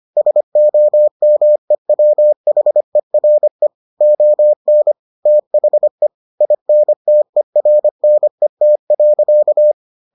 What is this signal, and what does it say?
Morse code, heard as a keyed tone, about 25 wpm, 600 Hz. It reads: SOMEWHERE ON THE INTERNET.